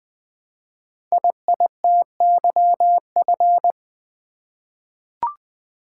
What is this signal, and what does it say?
Morse code, heard as a keyed tone, about 20 wpm, 700 Hz. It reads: IITYF
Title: Code Groups - 5 Letters - 20wpm